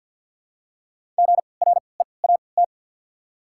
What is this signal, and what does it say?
Morse code, heard as a keyed tone, about 54 wpm, spaced as for 29 wpm, 700 Hz. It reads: GREAT
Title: 3x Character Spacing - Top 200 Words - 50wpm